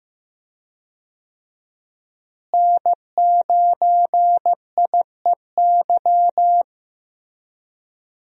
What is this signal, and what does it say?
Morse code, heard as a keyed tone, about 16 wpm, 700 Hz. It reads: N9IEY
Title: USA Call Signs - 5 Characters - Repeated 2x - 15wpm